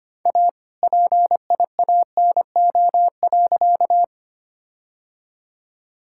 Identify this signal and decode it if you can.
Morse code, heard as a keyed tone, about 25 wpm, 700 Hz. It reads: A PIANO.
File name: practice / 25wpm / All-Age Appropriate Jokes - 25wpm